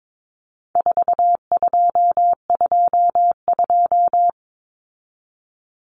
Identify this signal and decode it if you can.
Morse code, heard as a keyed tone, about 22 wpm, 700 Hz. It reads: 4222
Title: Common Cut Numbers - 4-Digit 22wpm